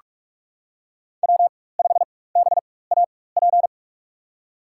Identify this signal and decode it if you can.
Morse code, heard as a keyed tone, about 47 wpm, spaced as for 21 wpm, 700 Hz. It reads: W5BAP
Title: USA Call Signs - 5 Characters - 4x Character Spacing - 45wpm